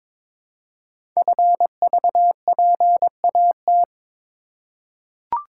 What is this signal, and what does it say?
Morse code, heard as a keyed tone, about 22 wpm, 700 Hz. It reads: FVPAT
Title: Code Groups - 5 Letters - 22wpm